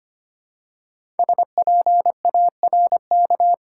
Morse code, heard as a keyed tone, about 25 wpm, 700 Hz. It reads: SPARK
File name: 5 Letter Words - First 3 Letters Given - 25wpm